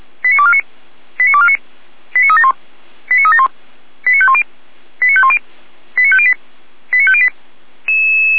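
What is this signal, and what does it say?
Five-tone selective call (ZVEI), heard as quick runs of five tones, tones of about 70 ms. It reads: 89249 89249 89371 89371 89510 89510 89508 89508
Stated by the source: ZVEI five-tone selective call 89558 received by SDR.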